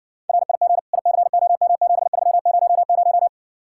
Morse code, heard as a keyed tone, about 60 wpm, 700 Hz. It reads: RIG ICOM7300